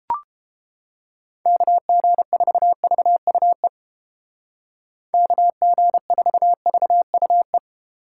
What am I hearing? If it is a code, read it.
Morse code, heard as a keyed tone, about 29 wpm, 700 Hz. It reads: KG4VUE KG4VUE